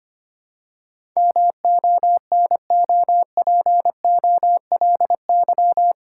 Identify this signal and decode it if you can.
Morse code, heard as a keyed tone, about 25 wpm, 690 Hz. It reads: MONOPOLY
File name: 8 Letter Words - First 3 Letters Given - 25wpm